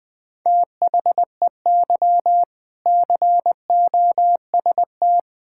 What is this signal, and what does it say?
Morse code, heard as a keyed tone, about 21 wpm, 700 Hz. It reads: THEY COST
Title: Sentences from Top 1000 Words - Review Part 5 - 20wpm